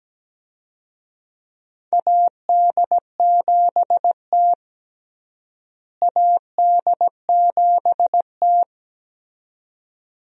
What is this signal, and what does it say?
Morse code, heard as a keyed tone, about 17 wpm, 700 Hz. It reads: AD7T AD7T
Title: USA Call Signs - 4 Characters - Repeated 2x - 17wpm